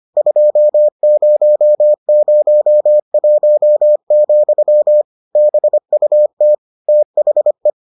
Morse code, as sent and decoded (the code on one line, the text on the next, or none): ..--- ----- ----- .---- --..-- / -... ..- - / - .... .
2001, BUT THE